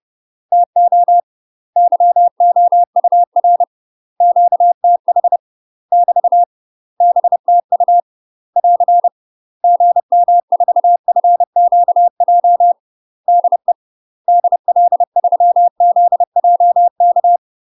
Morse code, heard as a keyed tone, about 30 wpm, 700 Hz. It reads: TO YOUR QTH <BT> BTU <AR> GM4FQJ DE DL3ZJK
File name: UK MASTER QSOs TRAINING / 30WPM / UK MASTER QSOs 0018